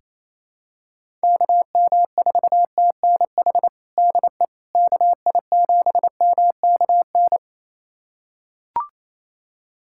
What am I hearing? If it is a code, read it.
Morse code, heard as a keyed tone, about 28 wpm, 700 Hz. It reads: KM4TNH DE KI7MKN
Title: USA Callsign DE Callsign - 6 Characters - 28wpm